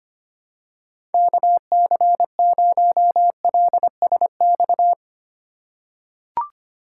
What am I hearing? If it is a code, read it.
Morse code, heard as a keyed tone, about 25 wpm, 700 Hz. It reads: KC0LSX
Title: USA Call Signs - 4x Character Spacing - 25wpm